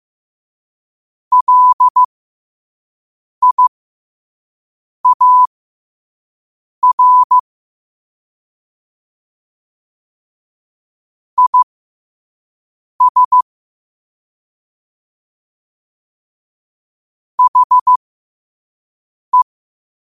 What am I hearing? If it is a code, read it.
Morse code, heard as a keyed tone, about 15 wpm, spaced as for 5 wpm, 1000 Hz. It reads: LIAR IS HE
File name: 5wpm